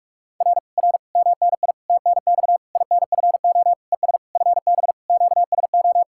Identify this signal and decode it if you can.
Morse code, heard as a keyed tone, about 49 wpm, 700 Hz. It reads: R R MNI TNX INFO ES FB QSO